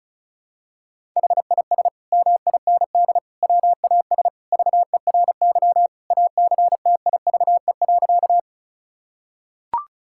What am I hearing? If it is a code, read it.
Morse code, heard as a keyed tone, about 36 wpm, 700 Hz. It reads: HIS MIND WAS VERY ACTIVE.
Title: Sentences from Top 1000 Words - Review Part 6 - 35wpm